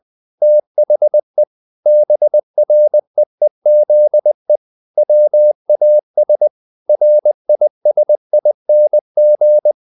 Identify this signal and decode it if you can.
Morse code, heard as a keyed tone, about 20 wpm, 600 Hz. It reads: THE BREEZE WAS RISING
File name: the old man and the sea / the old man and the sea part 1 20wpm